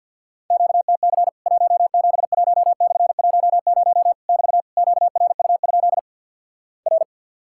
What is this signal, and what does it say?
Morse code, heard as a keyed tone, about 50 wpm, 700 Hz. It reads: QTC 171/10 <BT> QRU? R
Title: Worked All Europe DX Contest - QTC Exchange - 50wpm